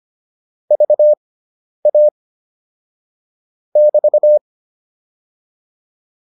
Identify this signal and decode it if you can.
Morse code, heard as a keyed tone, about 25 wpm, spaced as for 10 wpm, 600 Hz. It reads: VA <BT>
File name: QSO302 10